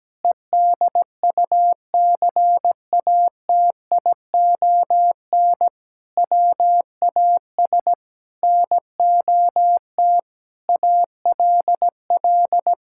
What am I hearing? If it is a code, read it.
Morse code, heard as a keyed tone, about 17 wpm, 700 Hz. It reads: EDUCATION WAS NOT ALL
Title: Sentences from Top 1000 Words - Review Part 6 - 17wpm